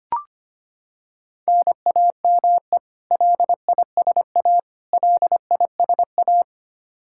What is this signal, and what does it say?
Morse code, heard as a keyed tone, about 25 wpm, 700 Hz. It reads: NAME LISA LISA
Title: Practice QSO - Exchange 1 - Beginning - 25wpm